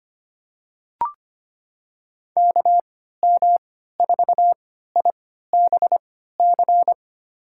Morse code, heard as a keyed tone, about 25 wpm, spaced as for 14 wpm, 700 Hz. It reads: KM4IBC
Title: USA Call Signs - 6 Characters - 3x Character Spacing - 25wpm